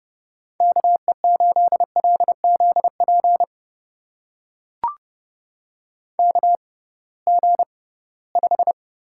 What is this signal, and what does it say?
Morse code, heard as a keyed tone, about 30 wpm, spaced as for 19 wpm, 700 Hz. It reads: KE8LZP KG5
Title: USA Call Signs - 6 Characters - 6x Character Spacing - 30wpm